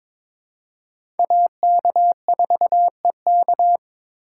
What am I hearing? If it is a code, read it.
Morse code, heard as a keyed tone, about 22 wpm, 700 Hz. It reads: AK4EK
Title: USA Call Signs - 3x Character Spacing - 22wpm